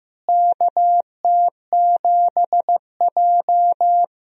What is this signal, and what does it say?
Morse code, heard as a keyed tone, about 16 wpm, 700 Hz. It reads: KT7J
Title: USA Call Signs - 4 Characters - Repeated 2x - 15wpm